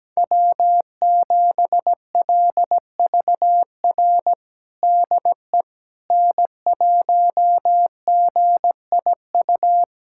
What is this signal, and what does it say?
Morse code, heard as a keyed tone, about 17 wpm, 700 Hz. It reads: W7LVR DE N1GIU
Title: USA Callsign DE Callsign - 5 Characters - 17wpm